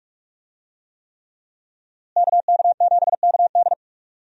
Morse code, heard as a keyed tone, about 46 wpm, 700 Hz. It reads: KK7KD
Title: USA Call Signs - 5 Characters - Repeated 3x - 45wpm